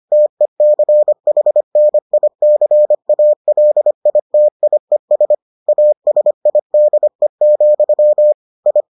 Morse code, heard as a keyed tone, about 25 wpm, 600 Hz. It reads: TECHNICALITIES ASIDE, I